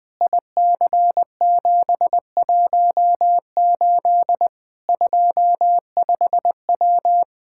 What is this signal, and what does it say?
Morse code, heard as a keyed tone, about 20 wpm, 700 Hz. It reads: IC718 25W
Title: K1USN SST QSO Practice 20wpm